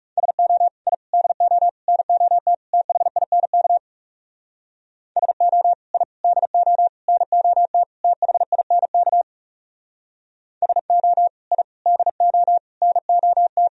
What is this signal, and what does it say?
Morse code, heard as a keyed tone, about 38 wpm, 700 Hz. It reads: SO I DO NOT THINK SO I DO NOT THINK SO I DO NOT